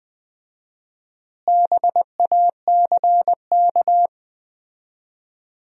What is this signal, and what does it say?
Morse code, heard as a keyed tone, about 20 wpm, 700 Hz. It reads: BACK
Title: 4x Word-Spacing - Sentences from Top 300 Words - 20wpm